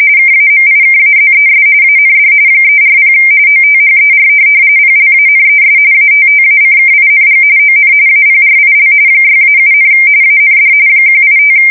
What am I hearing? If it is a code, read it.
RTTY, about 45.45 baud. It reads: KAMIKAZES AND 340 OTHER PLANES ATTACKING ON APRIL 6-7, AND 185
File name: fsk170 / USS-New-Jersey